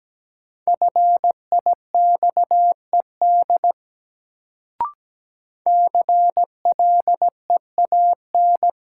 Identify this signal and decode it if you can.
Morse code, heard as a keyed tone, about 17 wpm, 700 Hz. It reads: FIXED CLEAN
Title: Sentences from Top 1100 Words - New Word Drill Alt - 17wpm